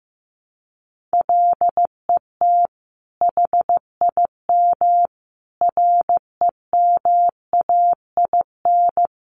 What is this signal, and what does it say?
Morse code, heard as a keyed tone, about 15 wpm, 700 Hz. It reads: LET HIM REMAIN
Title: Sentences from Top 1200 Words 15wpm